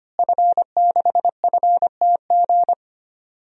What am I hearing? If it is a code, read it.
Morse code, heard as a keyed tone, about 25 wpm, 700 Hz. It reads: F6FTG
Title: DXCC-100 Call Signs - 5 Characters - Repeated 2x - 25wpm